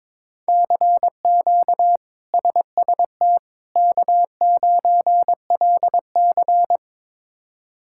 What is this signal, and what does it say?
Morse code, heard as a keyed tone, about 22 wpm, 700 Hz. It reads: CQ SST K9LC
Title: K1USN SST QSO Practice - 22wpm